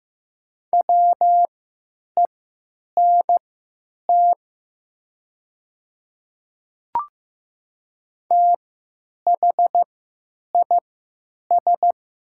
Morse code, heard as a keyed tone, about 15 wpm, spaced as for 9 wpm, 700 Hz. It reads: WENT THIS
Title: 3x Character Spacing - Top 200 Words - 15wpm